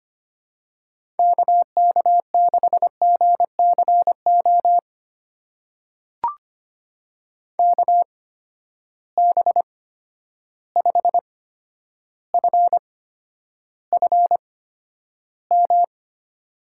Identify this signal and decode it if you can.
Morse code, heard as a keyed tone, about 25 wpm, spaced as for 11 wpm, 700 Hz. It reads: KK6GCO KB5FFM
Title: USA Call Signs - 6 Characters - 8x Character Spacing - 25wpm